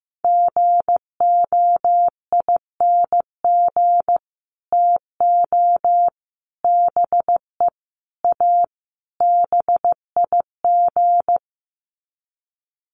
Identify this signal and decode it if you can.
Morse code, heard as a keyed tone, about 15 wpm, 700 Hz. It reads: GOING TO BE A BIG
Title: Sets of 5 Words Encore2 15wpm